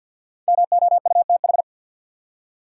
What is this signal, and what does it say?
Morse code, heard as a keyed tone, about 53 wpm, 700 Hz. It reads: MOUTH
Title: Five Letter Words - 50wpm